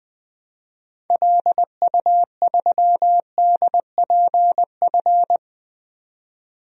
Morse code, heard as a keyed tone, about 20 wpm, 700 Hz. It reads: LU3DPF
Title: DXCC-100 Call Signs - 6 Characters - 20wpm